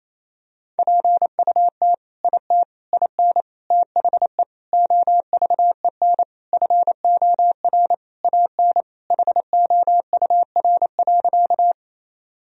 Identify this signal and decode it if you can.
Morse code, heard as a keyed tone, about 28 wpm, 700 Hz. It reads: PUT IT IN THE OVEN FOR AN HOUR.
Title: Sentences from Top 1000 Words - Review Part 4 - 28wpm